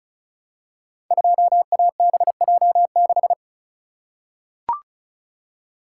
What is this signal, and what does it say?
Morse code, heard as a keyed tone, about 36 wpm, 700 Hz. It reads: 2ABJ6
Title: Code Groups - 5 Letter:Numbers - 35wpm